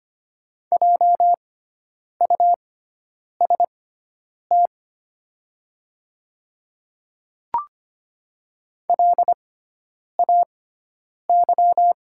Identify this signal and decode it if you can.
Morse code, heard as a keyed tone, about 25 wpm, spaced as for 9 wpm, 700 Hz. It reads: JUST LAY
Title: 6x Character Spacing - Top 500 Words - 25wpm